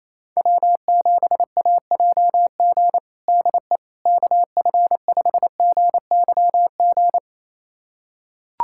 Morse code, heard as a keyed tone, about 28 wpm, 700 Hz. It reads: W7AJG DE KF5GYG
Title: USA Callsign DE Callsign 28wpm